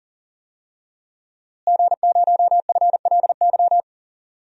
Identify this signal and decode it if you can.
Morse code, heard as a keyed tone, about 40 wpm, 700 Hz. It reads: G0FLY
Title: DXCC-100 Call Signs - 5 Characters - Repeated 3x - 40wpm